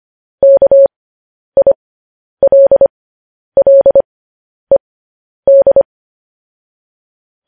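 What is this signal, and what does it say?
Morse code, heard as a keyed tone, about 25 wpm, spaced as for 10 wpm, 550 Hz. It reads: KILLED
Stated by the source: Morse code transmission of news headlines.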